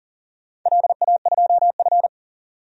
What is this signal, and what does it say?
Morse code, heard as a keyed tone, about 42 wpm, 700 Hz. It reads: LA2F
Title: DXCC-100 Call Signs - 4 Characters - 40wpm